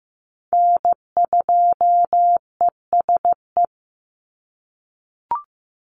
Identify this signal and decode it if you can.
Morse code, heard as a keyed tone, about 15 wpm, 700 Hz. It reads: N2ESE
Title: NA SOTA Chaser Callsigns 15wpm